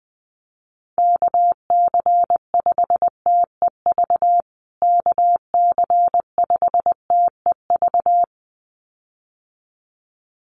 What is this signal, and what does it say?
Morse code, heard as a keyed tone, about 20 wpm, 700 Hz. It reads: KC5TEV KC5TEV
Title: USA Call Signs Repeated 2x 20wpm